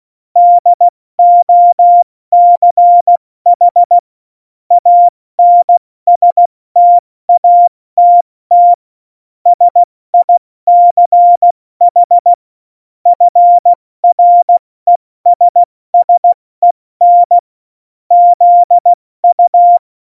Morse code, heard as a keyed tone, about 16 wpm, 700 Hz. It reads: DOCH ANSTATT SICH FRESSEN ZU
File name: Morsekurs / Witze / Witz 18. September 2018, 70 F80